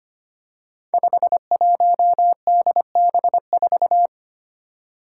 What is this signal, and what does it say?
Morse code, heard as a keyed tone, about 25 wpm, 700 Hz. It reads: 51DB4
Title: Five Challenging Letter-Number Combinations 25wpm